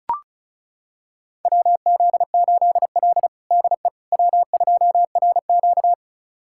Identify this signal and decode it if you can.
Morse code, heard as a keyed tone, about 36 wpm, 700 Hz. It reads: WZ8L DE W2RQ
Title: USA Callsign DE Callsign - 4 Characters - 35wpm